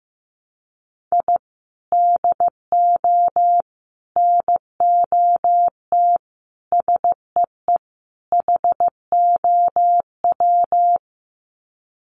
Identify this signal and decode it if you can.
Morse code, heard as a keyed tone, about 15 wpm, 700 Hz. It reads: I DO NOT SEE HOW
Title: Sets of 5 Words 15wpm